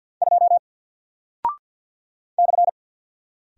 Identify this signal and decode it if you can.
Morse code, heard as a keyed tone, about 54 wpm, 700 Hz. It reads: 2 /
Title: Single Character (Letter:Number:Punctuation:Prosign) 50wpm